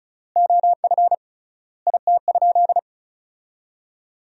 Morse code, heard as a keyed tone, about 34 wpm, 700 Hz. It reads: OF IT?
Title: 3x Word-Spacing - Sentences from Top 100 Words - 35wpm